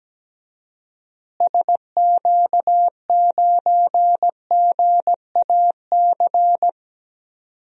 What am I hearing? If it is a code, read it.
Morse code, heard as a keyed tone, about 17 wpm, 700 Hz. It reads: SQ9GAC
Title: DXCC-100 Call Signs - 6 Characters - Repeated 3x - 17wpm